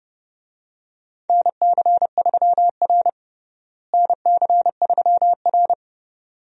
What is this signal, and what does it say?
Morse code, heard as a keyed tone, about 30 wpm, 700 Hz. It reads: NC3R NC3R
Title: USA Call Signs - 4 Characters - Repeated 2x - 30wpm